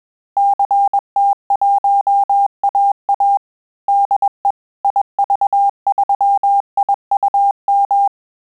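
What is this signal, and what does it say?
Morse code, heard as a keyed tone, about 21 wpm, 800 Hz. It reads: CT1AA DE IV3IUM
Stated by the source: Morse QSO practice at 20 words per minute, in Italian and English.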